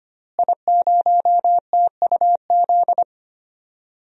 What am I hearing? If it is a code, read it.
Morse code, heard as a keyed tone, about 25 wpm, 700 Hz. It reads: I0TUZ
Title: DXCC-100 Call Signs - 5 Characters - Repeated 3x - 25wpm